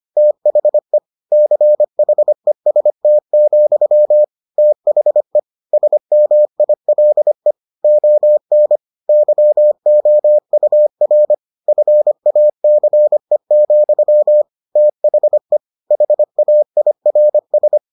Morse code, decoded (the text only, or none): THE CHEST, THE SMILE ON YOUR FACE, THE HAIRS